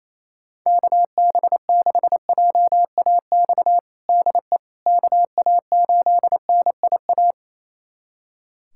Morse code, heard as a keyed tone, about 28 wpm, 700 Hz. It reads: KB6JAX DE KA8NIA
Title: USA Callsign DE Callsign - 6 Characters - 28wpm